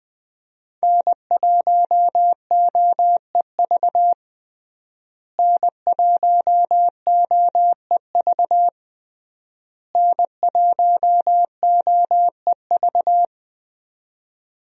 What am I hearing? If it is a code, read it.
Morse code, heard as a keyed tone, about 20 wpm, 700 Hz. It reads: N1OEV N1OEV N1OEV
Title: USA Call Signs - 5 Characters - Repeated 3x - 20wpm